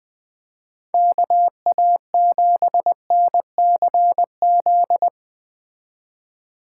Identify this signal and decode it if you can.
Morse code, heard as a keyed tone, about 19 wpm, 700 Hz. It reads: KA7NCZ
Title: Speed Racing - USA 2x3 Call Signs 17wpm